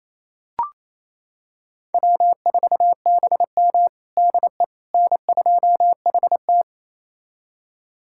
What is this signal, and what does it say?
Morse code, heard as a keyed tone, about 28 wpm, 700 Hz. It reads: W4BM DE N2HT
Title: USA Callsign DE Callsign - 4 Characters - 28wpm